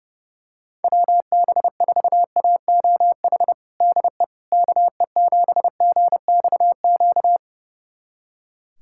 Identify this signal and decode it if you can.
Morse code, heard as a keyed tone, about 32 wpm, 700 Hz. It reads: WB4AOH DE KE7GXQ
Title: USA Callsign DE Callsign 30wpm